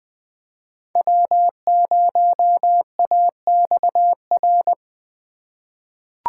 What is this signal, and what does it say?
Morse code, heard as a keyed tone, about 20 wpm, 700 Hz. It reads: W0AXR
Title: USA Call Signs - 3x Character Spacing - 20wpm